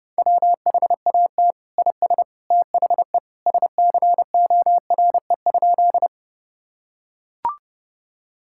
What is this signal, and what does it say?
Morse code, heard as a keyed tone, about 30 wpm, 700 Hz. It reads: WHAT IS THE SCORE?
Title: Sentences from Top 1600 Words 30wpm